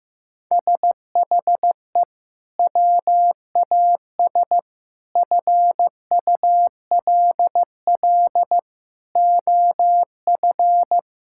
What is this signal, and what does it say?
Morse code, heard as a keyed tone, about 15 wpm, 700 Hz. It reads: SHE WAS FULL OF
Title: Sentences from Top 1000 Words - Review Part 6 - 15wpm